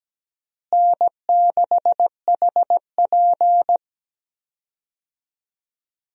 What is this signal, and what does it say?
Morse code, heard as a keyed tone, about 17 wpm, 700 Hz. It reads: N6HP
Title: USA Call Signs - 4 Characters - Repeated 2x - 17wpm